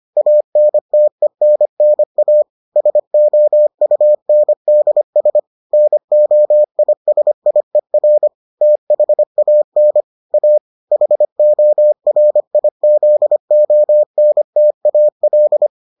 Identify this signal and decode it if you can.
Morse code, heard as a keyed tone, about 25 wpm, 600 Hz. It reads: ANTENNA SOUNDS NOISIER THAN A HORIZONTAL